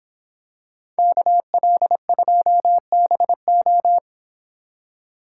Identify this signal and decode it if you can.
Morse code, heard as a keyed tone, about 27 wpm, 700 Hz. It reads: KL2BO
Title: Speed Racing - USA 2x2 Call Signs 22wpm